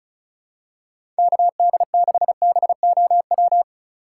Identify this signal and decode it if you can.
Morse code, heard as a keyed tone, about 36 wpm, 700 Hz. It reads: KD6BOW
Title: USA Call Signs - 4x Character Spacing - 35wpm